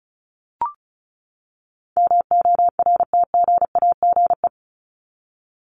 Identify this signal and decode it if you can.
Morse code, heard as a keyed tone, about 34 wpm, 700 Hz. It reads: MORTGAGE
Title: Top 1751-2000 Words 35wpm